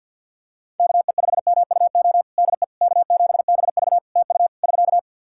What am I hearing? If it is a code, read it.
Morse code, heard as a keyed tone, about 50 wpm, 700 Hz. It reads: KE5MUO DE K7BV TU <SK>
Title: K1USN SST QSO Practice 50wpm